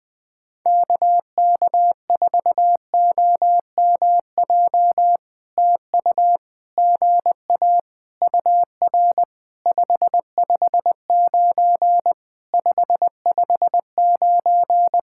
Morse code, heard as a keyed tone, about 20 wpm, 700 Hz. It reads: KK4OMJ TU GA UR 559 559
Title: POTA QSO Practice - 20wpm